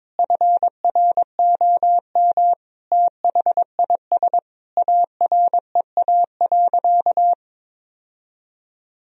Morse code, heard as a keyed tone, about 22 wpm, 700 Hz. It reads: FROM THIS AREA.
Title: Sentences from Top 1000 Words - Review Part 2 - 22wpm